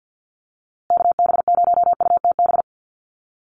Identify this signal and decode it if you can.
Morse code, heard as a keyed tone, about 50 wpm, 700 Hz. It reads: KB0UTB
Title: USA Call Signs Repeated 3x 50wpm